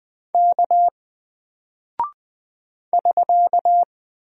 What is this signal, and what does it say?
Morse code, heard as a keyed tone, about 20 wpm, 700 Hz. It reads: K <SK>
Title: Single Character (Letter:Number:Punctuation:Prosign) 20wpm